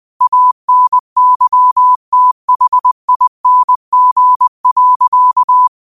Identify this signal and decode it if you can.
Morse code, heard as a keyed tone, about 20 wpm, 1000 Hz. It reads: ANYTHING.
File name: qotdfiles / 20wpm